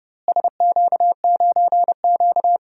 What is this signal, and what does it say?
Morse code, heard as a keyed tone, about 30 wpm, 700 Hz. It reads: SQ9Q
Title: DXCC-100 Call Signs - 4 Characters - Repeated 2x - 30wpm